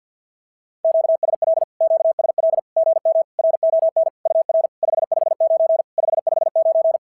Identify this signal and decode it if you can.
Morse code, heard as a keyed tone, about 46 wpm, 660 Hz. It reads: QSL QSL GM RON UR 559 559
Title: SOTA QSO Practice 50wpm